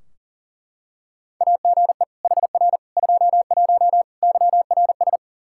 Morse code, heard as a keyed tone, about 40 wpm, 700 Hz. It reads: AGE HR 21 YRS
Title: Practice QSO - Exchange 3 - More Info - 40wpm